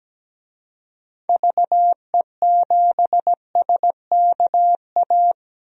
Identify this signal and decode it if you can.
Morse code, heard as a keyed tone, about 17 wpm, 700 Hz. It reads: VE7SKA
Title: DXCC-100 Call Signs - 6 Characters - 17wpm